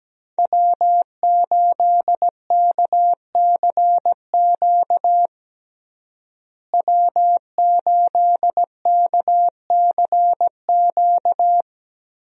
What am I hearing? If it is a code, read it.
Morse code, heard as a keyed tone, about 17 wpm, 700 Hz. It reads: W8KCQ W8KCQ
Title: USA Call Signs - 5 Characters - Repeated 2x - 17wpm